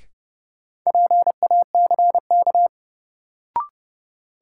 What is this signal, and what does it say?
Morse code, heard as a keyed tone, about 30 wpm, 700 Hz. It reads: PACK
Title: Sentences from Top 1300 Words - New Word Drill - 30wpm